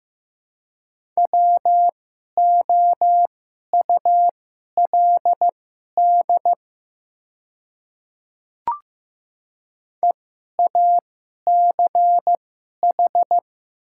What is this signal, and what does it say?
Morse code, heard as a keyed tone, about 15 wpm, spaced as for 11 wpm, 700 Hz. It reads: WOULD EACH
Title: 2x Character Spacing - Top 100 Words - 15wpm